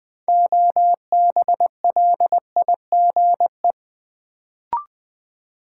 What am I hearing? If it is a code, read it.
Morse code, heard as a keyed tone, about 20 wpm, 700 Hz. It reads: OBLIGE
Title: Build A Word - 2 Letters At A Time - 6-Letter Words - 20wpm